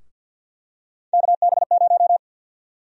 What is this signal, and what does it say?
Morse code, heard as a keyed tone, about 52 wpm, 700 Hz. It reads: KB0
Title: USA Call Sign Prefixes - 50wpm